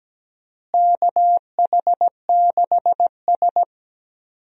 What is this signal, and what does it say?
Morse code, heard as a keyed tone, about 17 wpm, 700 Hz. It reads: KH6S
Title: DXCC-100 Call Signs - 4 Characters - Repeated 3x - 17wpm